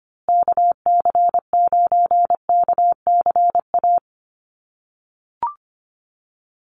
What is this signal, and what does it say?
Morse code, heard as a keyed tone, about 24 wpm, 700 Hz. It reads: KC9KCA
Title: USA Call Signs Repeated 3x 25wpm